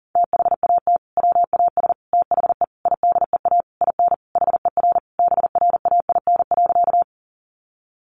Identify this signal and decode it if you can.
Morse code, heard as a keyed tone, about 40 wpm, 700 Hz. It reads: THAT WAS THE IDEA IN HER BRAIN.
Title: Sentences from Top 1100 Words 40wpm